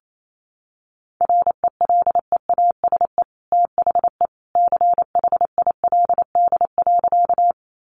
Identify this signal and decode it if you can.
Morse code, heard as a keyed tone, about 28 wpm, 700 Hz. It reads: RELEASE THE CHILD.
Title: Sentences from Top 1100 Words 28wpm